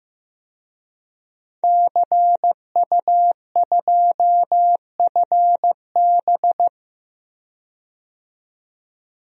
Morse code, heard as a keyed tone, about 15 wpm, 700 Hz. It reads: CU2FB
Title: DXCC-100 Call Signs - 5 Characters - 15wpm